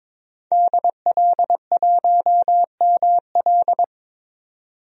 Morse code, heard as a keyed tone, about 22 wpm, 700 Hz. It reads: DL1ML
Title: DXCC-100 Call Signs - 5 Characters - 22wpm